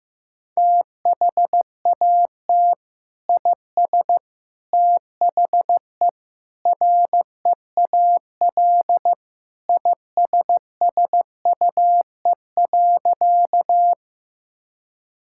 Morse code, heard as a keyed tone, about 15 wpm, 700 Hz. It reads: THAT IS THE REAL ISSUE.
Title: Sentences from Top 1000 Words - Review Part 6 - 15wpm